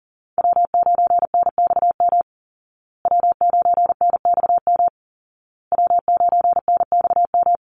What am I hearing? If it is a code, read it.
Morse code, heard as a keyed tone, about 40 wpm, 700 Hz. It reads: W9NXM W9NXM W9NXM
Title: Call Signs Repeated 2x - 40wpm